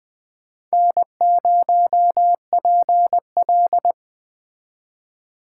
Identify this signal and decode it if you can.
Morse code, heard as a keyed tone, about 21 wpm, 700 Hz. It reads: N0PL
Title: Speed Racing - USA 1x2 Call Signs 17wpm